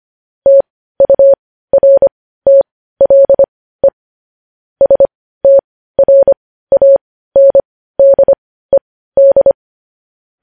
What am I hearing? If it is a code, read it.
Morse code, heard as a keyed tone, about 25 wpm, spaced as for 15 wpm, 550 Hz. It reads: TURTLE STRANDED
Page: news-headlines-in-morse-code-at-15-wpm